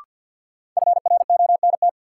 Morse code, heard as a keyed tone, about 53 wpm, 700 Hz. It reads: FRONT
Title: Sets of 2 Words - New Word Drill - 50wpm